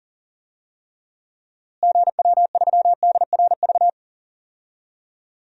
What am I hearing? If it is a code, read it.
Morse code, heard as a keyed tone, about 40 wpm, 700 Hz. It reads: GW3DRV
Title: DXCC-100 Call Signs - 6 Characters - Repeated 3x - 40wpm